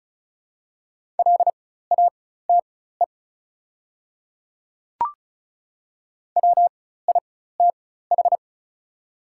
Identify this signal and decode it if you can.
Morse code, heard as a keyed tone, about 36 wpm, spaced as for 16 wpm, 690 Hz. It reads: LATE WITH
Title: 4x Character Spacing - Top 300 Words - 35wpm